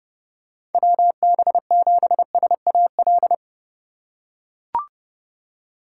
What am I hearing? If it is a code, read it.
Morse code, heard as a keyed tone, about 32 wpm, 700 Hz. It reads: WB7SAL
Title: USA Call Signs - 3x Character Spacing - 30wpm